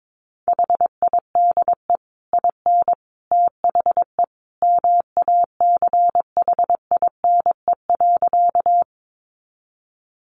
Morse code, heard as a keyed tone, about 22 wpm, 700 Hz. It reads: HIDE IN THE MACHINE.
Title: A Sound Of Thunder 22wpm